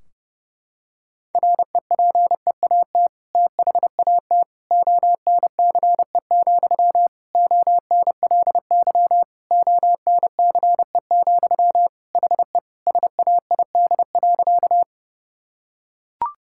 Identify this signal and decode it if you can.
Morse code, heard as a keyed tone, about 30 wpm, 700 Hz. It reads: REPEAT THAT ONCE, ONLY ONCE, HE SAID.
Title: Sentences from Top 1600 Words 30wpm